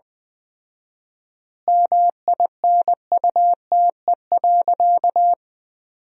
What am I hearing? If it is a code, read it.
Morse code, heard as a keyed tone, about 20 wpm, 700 Hz. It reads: MINUTE.
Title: 4x Word-Spacing - Sentences from Top 500 Words - 20wpm